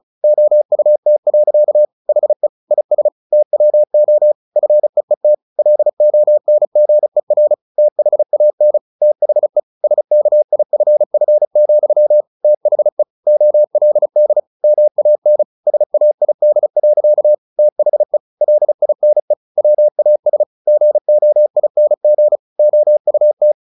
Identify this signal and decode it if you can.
Morse code, heard as a keyed tone, about 34 wpm, 600 Hz. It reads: OUT. HE IS TWO FEET LONGER THAN THE SKIFF, THE OLD MAN SAID. THE LINE WAS GOING OUT
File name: the old man and the sea / the old man and the sea 1 35wpm